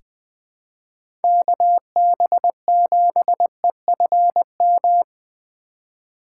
Morse code, heard as a keyed tone, about 20 wpm, 700 Hz. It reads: KB7EFM
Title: USA Call Signs - 3x Character Spacing - 20wpm